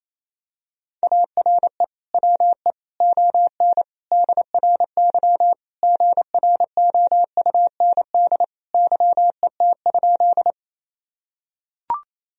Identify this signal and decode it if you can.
Morse code, heard as a keyed tone, about 28 wpm, 700 Hz. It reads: ARE WE ON DRY GROUND YET?
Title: Sentences from Top 1000 Words - Review Part 3 - 28wpm